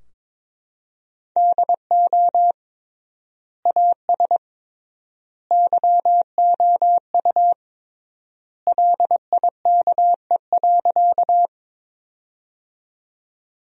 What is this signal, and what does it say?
Morse code, heard as a keyed tone, about 22 wpm, 700 Hz. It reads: DO AS YOU LIKE.
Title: 3x Word-Spacing - Sentences from Top 100 Words - 22wpm